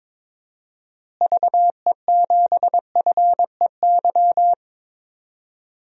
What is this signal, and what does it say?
Morse code, heard as a keyed tone, about 22 wpm, 700 Hz. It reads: VE7FEY
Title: DXCC-100 Call Signs - 6 Characters - Repeated 3x - 22wpm